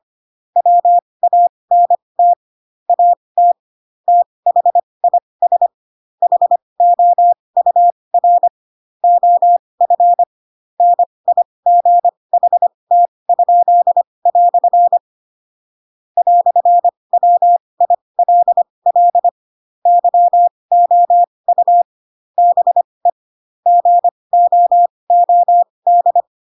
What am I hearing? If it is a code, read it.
Morse code, heard as a keyed tone, about 25 wpm, spaced as for 20 wpm, 700 Hz. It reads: WANT AT THIS HOUR OF NIGHT?" "WILL YOU BE GOOD
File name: Pinocchio / Pinocchio ch6 15 25 ik1hge